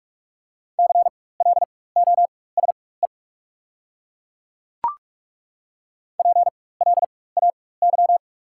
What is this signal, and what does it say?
Morse code, heard as a keyed tone, about 43 wpm, spaced as for 21 wpm, 690 Hz. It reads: CLOSE PLAY
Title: 4x Character Spacing - Top 300 Words - 45wpm